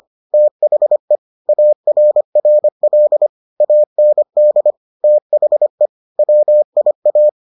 Morse code, heard as a keyed tone, about 25 wpm, 600 Hz. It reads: THE ARRL AND THE WIA